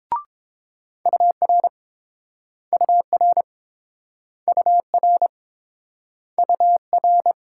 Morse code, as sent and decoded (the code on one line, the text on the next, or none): ..- .-. / ..- .-. / ..- .-. / ..- .-.
UR UR UR UR